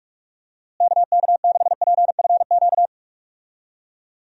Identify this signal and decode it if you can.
Morse code, heard as a keyed tone, about 46 wpm, 700 Hz. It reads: KK6PFQ
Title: USA Call Signs - 6 Characters - Repeated 3x - 45wpm